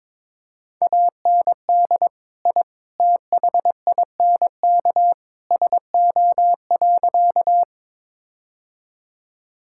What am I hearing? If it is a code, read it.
Morse code, heard as a keyed tone, about 21 wpm, 700 Hz. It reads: AND I THINK SO.
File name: practice / 22wpm / Sentences with 3-Word Phrases - 22wpm